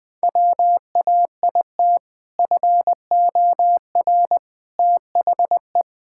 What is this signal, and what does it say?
Morse code, heard as a keyed tone, about 20 wpm, 700 Hz. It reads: WAIT FOR THE
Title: Sentences with 2-Word Phrases - Spoken Beginning - 20wpm